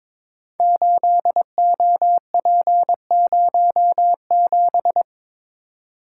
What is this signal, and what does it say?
Morse code, heard as a keyed tone, about 22 wpm, 700 Hz. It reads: 8OP07
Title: Code Groups - 5 Letter:Numbers - 22wpm